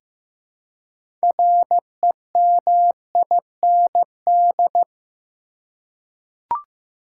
Morse code, heard as a keyed tone, about 15 wpm, 700 Hz. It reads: REMIND
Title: Six Letter Words - 15wpm